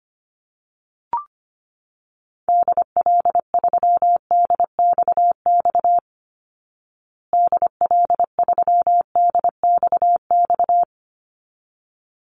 Morse code, heard as a keyed tone, about 25 wpm, 700 Hz. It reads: DL3DXX DL3DXX
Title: Call Signs Repeated 3x - 25wpm